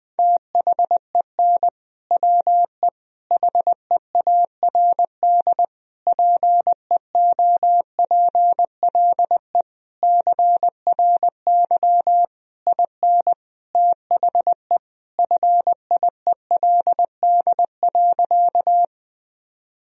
Morse code, heard as a keyed tone, about 21 wpm, 700 Hz. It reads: THEN WE HEARD PEOPLE CRY IN THE FIELD.
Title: Sentences from Top 1000 Words - Review Part 2 - 20wpm